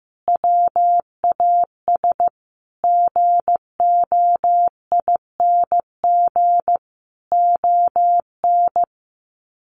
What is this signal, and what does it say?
Morse code, heard as a keyed tone, about 15 wpm, 700 Hz. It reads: WAS GOING ON
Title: Sets of 5 Words Encore2 15wpm